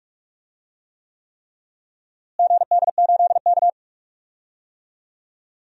Morse code, heard as a keyed tone, about 46 wpm, 700 Hz. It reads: GD8K
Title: DXCC-100 Call Signs - 4 Characters - Repeated 3x - 45wpm